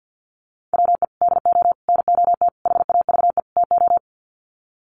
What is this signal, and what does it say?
Morse code, heard as a keyed tone, about 50 wpm, 700 Hz. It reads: WE DO NOT HAVE TO
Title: Sets of 5 Words 50wpm